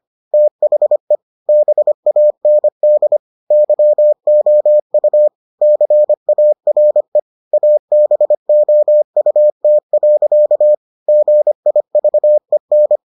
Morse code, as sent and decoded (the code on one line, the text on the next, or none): - .... / -... .- -. -.. / -.-- --- ..- / -.-. .- .-. / .- -... --- ..- - .-.-.- / --. .. ...- . -.
THE BAND YOU CARE ABOUT. GIVEN